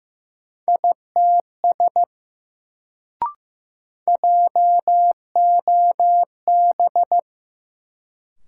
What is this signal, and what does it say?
Morse code, heard as a keyed tone, about 15 wpm, 700 Hz. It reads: ITS JOB
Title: Sentences from Top 600 Words - New Word Drill - 15wpm